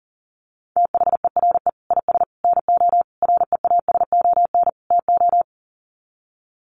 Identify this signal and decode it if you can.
Morse code, heard as a keyed tone, about 40 wpm, 700 Hz. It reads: THERE IS NO REASON TO
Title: Sets of 5 Words Encore 40wpm